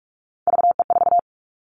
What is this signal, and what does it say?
Morse code, heard as a keyed tone, about 44 wpm, 700 Hz. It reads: VE4
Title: Callsign Prefixes for Top 50 DXCC - 45wpm